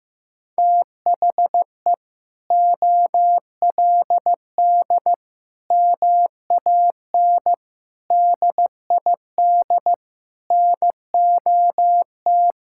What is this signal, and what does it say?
Morse code, heard as a keyed tone, about 16 wpm, 700 Hz. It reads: THE OLD MAN DID NOT
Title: Sentences from Top 1000 Words - Review Part 6 - 15wpm